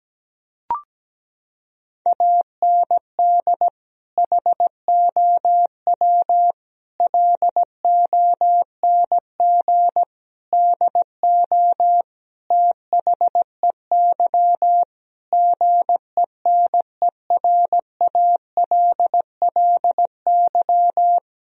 Morse code, heard as a keyed tone, about 17 wpm, 700 Hz. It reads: AND HOW LONG DO THEY GENERALLY
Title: Sentences from Top 1000 Words - Review Part 4 - 17wpm